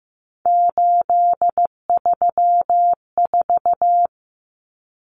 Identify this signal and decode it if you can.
Morse code, heard as a keyed tone, about 15 wpm, 700 Hz. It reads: 834
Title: Numbers - 3-Digit 15wpm